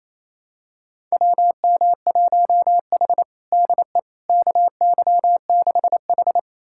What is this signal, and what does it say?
Morse code, heard as a keyed tone, about 28 wpm, 700 Hz. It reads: WM1H DE KY6H
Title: USA Callsign DE Callsign - 4 Characters - 28wpm